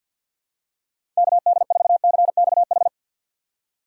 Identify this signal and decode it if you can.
Morse code, heard as a keyed tone, about 53 wpm, 700 Hz. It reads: KD4CXH
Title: USA Call Signs - 6 Characters - 4x Character Spacing - 50wpm